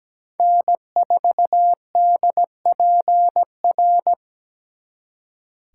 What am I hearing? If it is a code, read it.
Morse code, heard as a keyed tone, about 20 wpm, 700 Hz. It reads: N4DPR
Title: Speed Racing - USA 1x3 Call Signs 17wpm